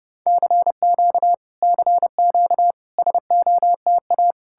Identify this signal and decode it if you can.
Morse code, heard as a keyed tone, about 34 wpm, 700 Hz. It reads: CQ CQ SOTA